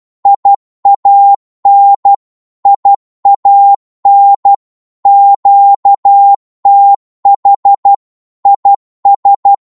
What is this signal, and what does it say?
Morse code, heard as a keyed tone, about 12 wpm, 800 Hz. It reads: IAN IAN QTH IS
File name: Morse / qso-12wpm